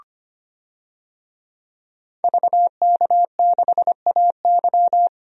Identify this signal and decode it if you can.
Morse code, heard as a keyed tone, about 25 wpm, 700 Hz. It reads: VK6AY
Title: DXCC-100 Call Signs - 5 Characters - 25wpm